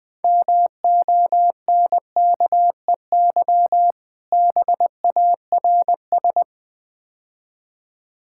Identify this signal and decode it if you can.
Morse code, heard as a keyed tone, about 22 wpm, 700 Hz. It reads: MONKEY BARS
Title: Easy Words - Speed-Racing with Hints - 20wpm